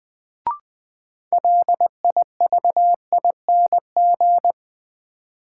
Morse code, heard as a keyed tone, about 20 wpm, 700 Hz. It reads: LIVING
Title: Sets of 4 Words - New Word Drill - 20wpm